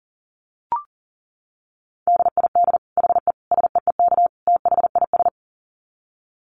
Morse code, heard as a keyed tone, about 37 wpm, 700 Hz. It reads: DID HE SEEK THIS
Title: Dit Challenge 40wpm